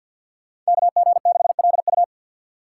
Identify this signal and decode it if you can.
Morse code, heard as a keyed tone, about 46 wpm, 700 Hz. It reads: KG6LU
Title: USA Call Signs - 5 Characters - 6x Character Spacing - 50wpm